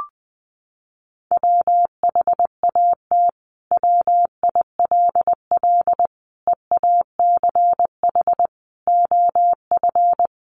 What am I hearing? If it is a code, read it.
Morse code, heard as a keyed tone, about 19 wpm, 700 Hz. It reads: WHAT WILL EACH OF
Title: Sentences from Top 200 Words 20wpm